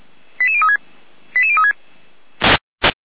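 Five-tone selective call (ZVEI), heard as quick runs of five tones, tones of about 70 ms. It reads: 88036 88036
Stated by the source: ZVEI five-tone selective call 88569 received via SDR.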